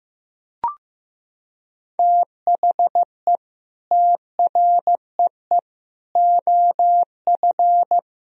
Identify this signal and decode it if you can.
Morse code, heard as a keyed tone, about 15 wpm, 700 Hz. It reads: THE TREE OF
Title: Sentences from Top 1000 Words - Review Part 1 - 15wpm